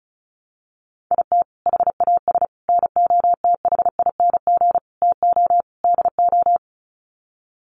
Morse code, heard as a keyed tone, about 34 wpm, 700 Hz. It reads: IT HAS NOTHING TO DO
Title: Sets of 5 Words Encore 35wpm